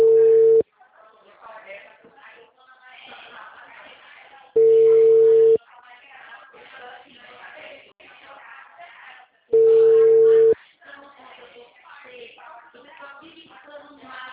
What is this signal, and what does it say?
Ringback tone (China, 450 Hz), with speech.